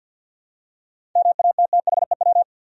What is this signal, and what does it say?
Morse code, heard as a keyed tone, about 52 wpm, 700 Hz. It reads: MATTHEW
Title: USA Common Names - 50wpm